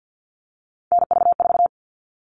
Morse code, heard as a keyed tone, about 50 wpm, 700 Hz. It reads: NV4
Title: All Cut Numbers - 3-Digit 50wpm